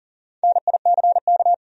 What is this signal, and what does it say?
Morse code, heard as a keyed tone, about 37 wpm, 700 Hz. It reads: NICK
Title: USA Common Names - Max 5 Letters - 40wpm